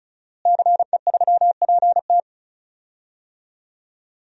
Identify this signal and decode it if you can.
Morse code, heard as a keyed tone, about 34 wpm, 700 Hz. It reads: CE3PT
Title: DXCC-100 Call Signs - 5 Characters - Repeated 2x - 35wpm